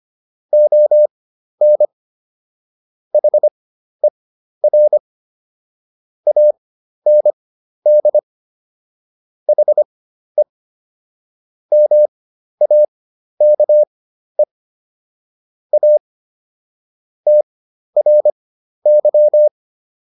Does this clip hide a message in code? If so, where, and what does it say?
Yes: Morse code from 0.5 to 19.5 s, heard as a keyed tone, about 25 wpm, spaced as for 12 wpm, 600 Hz. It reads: ON HER AND HE MAKE A TRY